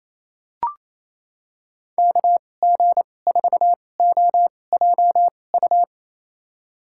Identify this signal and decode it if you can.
Morse code, heard as a keyed tone, about 28 wpm, spaced as for 20 wpm, 700 Hz. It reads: KG4OJU
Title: USA Call Signs - 6 Characters - 2x Character Spacing - 28wpm